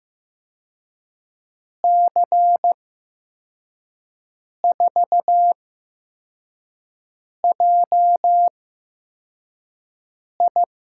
Morse code, heard as a keyed tone, about 15 wpm, spaced as for 4 wpm, 700 Hz. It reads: C4JI
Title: USA Call Signs - 8x Character Spacing - 15wpm